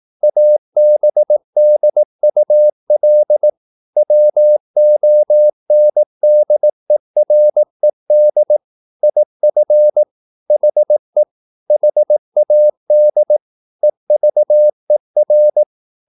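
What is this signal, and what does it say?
Morse code, heard as a keyed tone, about 18 wpm, 600 Hz. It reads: ABDUL WONDERED IF HE HAD EVER
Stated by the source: Morse practice at 18 wpm, English text, from CW med Gnister.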